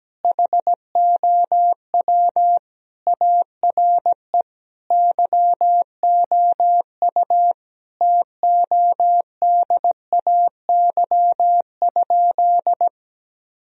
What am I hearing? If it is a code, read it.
Morse code, heard as a keyed tone, about 17 wpm, 700 Hz. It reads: HOW ARE YOU TODAY?
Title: Sentences from Top 1000 Words - Review Part 3 - 17wpm